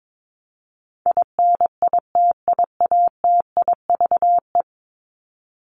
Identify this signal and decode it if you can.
Morse code, heard as a keyed tone, about 22 wpm, 700 Hz. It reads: INITIATIVE
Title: Top 1751-2000 Words 22wpm